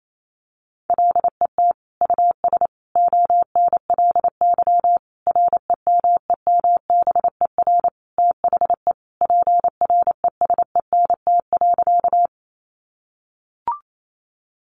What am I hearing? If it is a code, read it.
Morse code, heard as a keyed tone, about 28 wpm, 700 Hz. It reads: LET US ONLY REMEMBER THE PRESENT.
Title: Sentences from Top 800 Words 28wpm